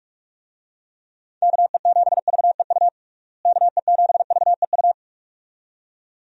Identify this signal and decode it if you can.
Morse code, heard as a keyed tone, about 41 wpm, 700 Hz. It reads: KE7VEU KE7VEU
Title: USA Call Signs - 6 Characters - Repeated 2x - 45wpm